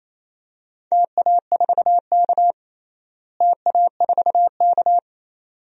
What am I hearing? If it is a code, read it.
Morse code, heard as a keyed tone, about 28 wpm, 700 Hz. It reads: TA4K TA4K
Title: DXCC-100 Call Signs - 4 Characters - Repeated 2x - 28wpm